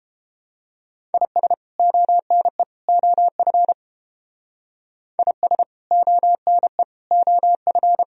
Morse code, heard as a keyed tone, about 28 wpm, 700 Hz. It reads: IS ONE OF IS ONE OF